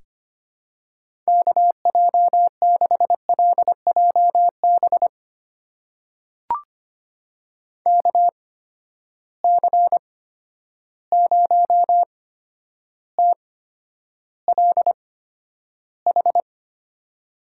Morse code, heard as a keyed tone, about 25 wpm, spaced as for 11 wpm, 700 Hz. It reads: KJ6LJB KC0TLH
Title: USA Call Signs - 6 Characters - 8x Character Spacing - 25wpm